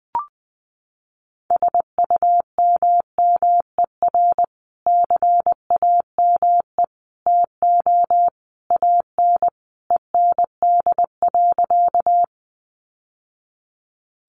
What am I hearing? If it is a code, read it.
Morse code, heard as a keyed tone, about 19 wpm, 700 Hz. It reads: SUMMER CAME TO AN END.
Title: Sentences from Top 1100 Words 20wpm